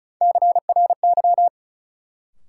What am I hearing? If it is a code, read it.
Morse code, heard as a keyed tone, about 36 wpm, 700 Hz. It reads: CRY
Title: Sentences from Top 500 Words - New Word Drill - 35wpm